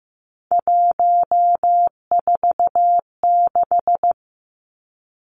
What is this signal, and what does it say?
Morse code, heard as a keyed tone, about 15 wpm, 700 Hz. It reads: 146
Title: Three Challenging Letter-Number Combinations 15wpm